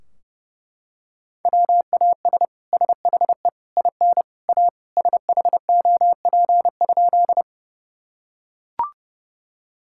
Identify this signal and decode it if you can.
Morse code, heard as a keyed tone, about 30 wpm, 700 Hz. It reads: WAS SHE IN A SHOP?
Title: Sentences from Top 1600 Words 30wpm